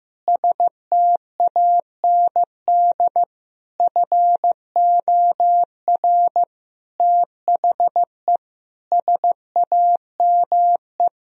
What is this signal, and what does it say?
Morse code, heard as a keyed tone, about 15 wpm, 700 Hz. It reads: STAND FOR THE SAME
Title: Sentences from Top 1000 Words - Review Part 3 - 15wpm